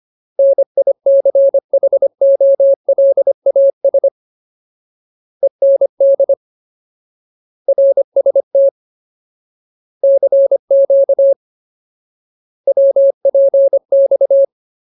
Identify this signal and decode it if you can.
Morse code, heard as a keyed tone, about 25 wpm, 550 Hz. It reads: NICHOLAS END RST CQ WPX
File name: Ultimate Word List / 25WPM Ultimate